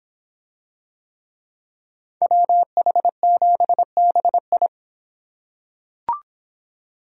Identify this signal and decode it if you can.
Morse code, heard as a keyed tone, about 23 wpm, 700 Hz. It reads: WH7BI